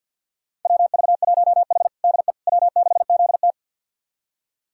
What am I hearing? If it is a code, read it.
Morse code, heard as a keyed tone, about 52 wpm, 700 Hz. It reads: WV1H DE W6ZT
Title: USA Callsign DE Callsign - 4 Characters - 50wpm